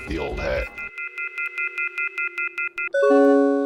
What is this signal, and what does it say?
Off-hook warning tone, 1400 + 2060 + 2450 + 2600 Hz, with speech.